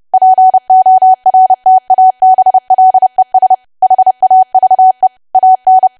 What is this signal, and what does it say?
Morse code, heard as a keyed tone, about 30 wpm, 750 Hz. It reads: PORTABLES HAVE AN
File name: cwa / W1AW-Level3 / 30WPM